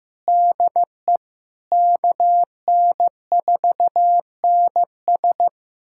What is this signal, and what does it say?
Morse code, heard as a keyed tone, about 16 wpm, 700 Hz. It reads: DE KN4NS
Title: USA Callsign DE Callsign - 5 Characters - 15wpm